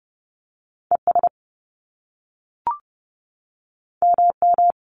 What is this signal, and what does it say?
Morse code, heard as a keyed tone, about 30 wpm, 700 Hz. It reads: ES MM
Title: Top Two Letter Combinations in Words 30wpm